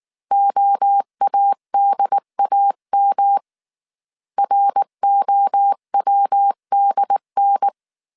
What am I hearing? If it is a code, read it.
Morse code, heard as a keyed tone, about 19 wpm, 800 Hz. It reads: OADAM ROWDN
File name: CW / DET / 1 3 1500C S B 18wpm 1916